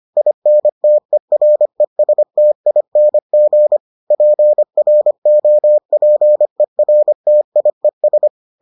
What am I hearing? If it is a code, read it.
Morse code, heard as a keyed tone, about 25 wpm, 600 Hz. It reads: INTERESTING PROPERTIES